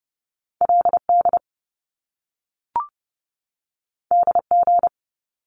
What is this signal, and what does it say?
Morse code, heard as a keyed tone, about 30 wpm, 700 Hz. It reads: LD DG
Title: Top Two Letter Combinations in Words 30wpm